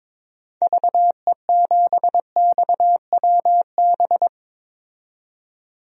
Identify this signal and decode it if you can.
Morse code, heard as a keyed tone, about 22 wpm, 700 Hz. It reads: VE7XWB
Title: DXCC-100 Call Signs - 6 Characters - 22wpm